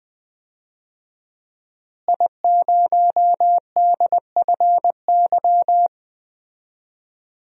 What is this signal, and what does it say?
Morse code, heard as a keyed tone, about 21 wpm, 700 Hz. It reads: I0DFY
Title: DXCC-100 Call Signs - 5 Characters - Repeated 3x - 20wpm